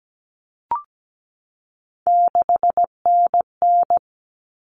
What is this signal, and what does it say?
Morse code, heard as a keyed tone, about 17 wpm, 700 Hz. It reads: 6NN
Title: Common Cut Numbers - 3-Digit 17wpm